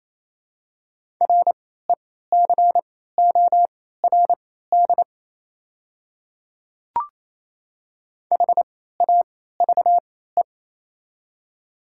Morse code, heard as a keyed tone, about 28 wpm, spaced as for 16 wpm, 700 Hz. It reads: RECORD HAVE
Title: 3x Character Spacing - Top 500 Words - 28wpm